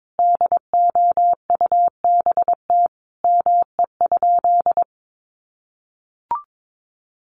Morse code, heard as a keyed tone, about 22 wpm, 700 Hz. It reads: DOUBT ME?
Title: Sentences from Top 1500 Words 22wpm